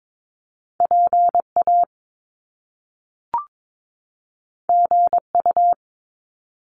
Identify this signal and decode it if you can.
Morse code, heard as a keyed tone, about 22 wpm, 700 Hz. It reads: PA GU
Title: Top Two Letter Combinations in Words 22wpm